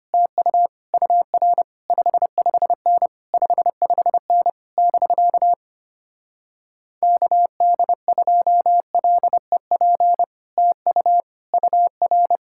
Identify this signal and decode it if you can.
Morse code, heard as a keyed tone, about 30 wpm, 700 Hz. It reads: TU UR 55N 55N <BK> KD2LEP TU UR
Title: Speed-Racing Common QSO Phrases - 25wpm